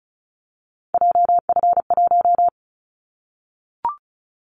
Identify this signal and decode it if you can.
Morse code, heard as a keyed tone, about 34 wpm, 700 Hz. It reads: JF1
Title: Callsign Prefixes for Top 25 DXCC - Repeated 3x - 35wpm